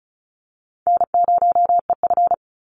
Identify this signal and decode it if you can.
Morse code, heard as a keyed tone, about 34 wpm, 700 Hz. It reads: N0EF
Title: Easy Call Signs Repeated 2x - 35wpm